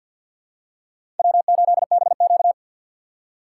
Morse code, heard as a keyed tone, about 53 wpm, 700 Hz. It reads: W8BQ
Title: USA Call Signs - 3x Character Spacing - 50wpm